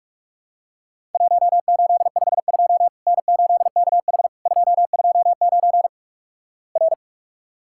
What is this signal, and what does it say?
Morse code, heard as a keyed tone, about 46 wpm, 700 Hz. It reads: 1852 N8KH 229 R
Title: Worked All Europe DX Contest - QTC Exchange - 45wpm